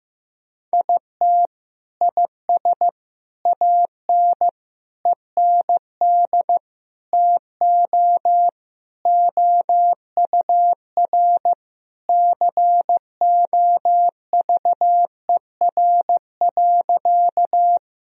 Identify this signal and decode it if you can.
Morse code, heard as a keyed tone, about 15 wpm, 700 Hz. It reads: IT IS AN END TO OUR COVER.
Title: Sentences from Top 1000 Words - Review Part 1 - 15wpm